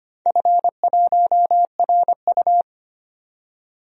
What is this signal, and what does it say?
Morse code, heard as a keyed tone, about 25 wpm, 700 Hz. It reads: F1RU
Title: DXCC-100 Call Signs - 4 Characters - 25wpm